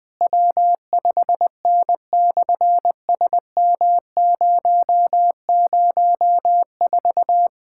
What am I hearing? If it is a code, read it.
Morse code, heard as a keyed tone, about 20 wpm, 700 Hz. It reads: W5N/SM004
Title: SOTA QSO Practice 20wpm